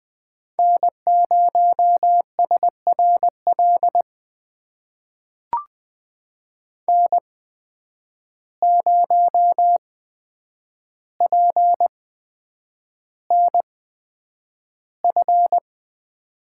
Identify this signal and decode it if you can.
Morse code, heard as a keyed tone, about 20 wpm, spaced as for 9 wpm, 700 Hz. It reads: N0SRL N0PNF
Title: USA Call Signs - 5 Characters - 8x Character Spacing - 20wpm